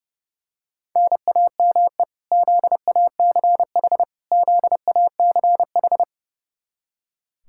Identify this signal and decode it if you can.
Morse code, heard as a keyed tone, about 30 wpm, 700 Hz. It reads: NAME ZACH ZACH
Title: Practice QSO - Exchange 1 - Beginning - 30wpm